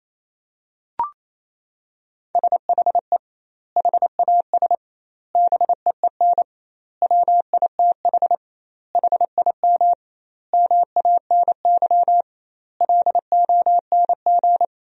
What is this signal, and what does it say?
Morse code, heard as a keyed tone, about 28 wpm, 700 Hz. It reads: SHE HAS BEEN WITH HIM MANY LONG
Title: 2x Word-Spacing - Sentences from Top 200 Words - 28wpm